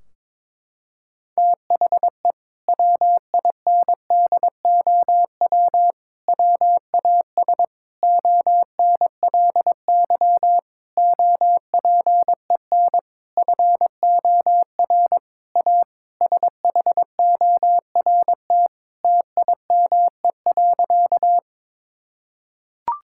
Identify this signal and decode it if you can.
Morse code, heard as a keyed tone, about 22 wpm, 700 Hz. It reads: THE WINDOW WAS ONLY OPEN FOR A SHORT TIME.
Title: Sentences from Top 1600 Words 22wpm